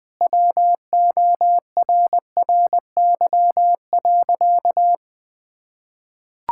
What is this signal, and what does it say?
Morse code, heard as a keyed tone, about 21 wpm, 700 Hz. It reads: WORRY.
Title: Pangram Challenge v2 - 20wpm